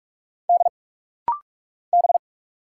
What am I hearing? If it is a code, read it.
Morse code, heard as a keyed tone, about 46 wpm, 700 Hz. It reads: D B
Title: Single Letter-Number Alt - Rapid-Fire 45wpm